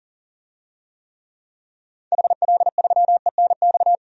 Morse code, heard as a keyed tone, about 37 wpm, 700 Hz. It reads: HL3ENX
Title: DXCC-100 Call Signs - 6 Characters - 40wpm